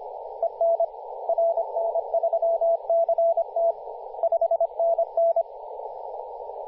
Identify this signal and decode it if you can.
Morse code, heard as a keyed tone, about 25 wpm, 670 Hz, with radio noise. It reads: EN RN3CT 5NN
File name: posts / 2016-july-07-2008z-v63gg-10-116mhz-cw